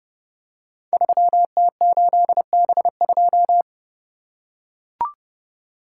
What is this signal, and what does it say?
Morse code, heard as a keyed tone, about 30 wpm, 700 Hz. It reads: 3T8B2
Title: Code Groups - 5 Letter:Numbers - 30wpm